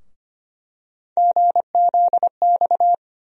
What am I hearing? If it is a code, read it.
Morse code, heard as a keyed tone, about 25 wpm, 700 Hz. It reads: GZX
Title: USA Call Sign Postfixes - 3 Letters 25wpm